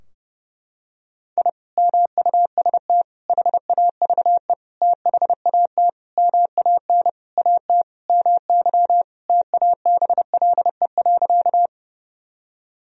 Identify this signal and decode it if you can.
Morse code, heard as a keyed tone, about 30 wpm, 700 Hz. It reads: I MUST HAVE THAT MAN AT MY TABLE.
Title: Sentences from Top 1000 Words - Review Part 2 - 30wpm